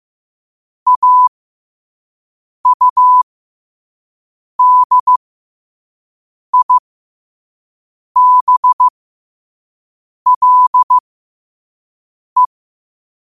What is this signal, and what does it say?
Morse code, heard as a keyed tone, about 15 wpm, spaced as for 5 wpm, 1000 Hz. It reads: AUDIBLE
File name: qotdfiles / 5wpm